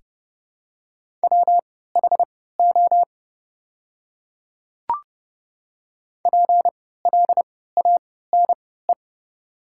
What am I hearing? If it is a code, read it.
Morse code, heard as a keyed tone, about 32 wpm, spaced as for 17 wpm, 700 Hz. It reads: WHO PLANE
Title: 3x Character Spacing - Top 500 Words - 30wpm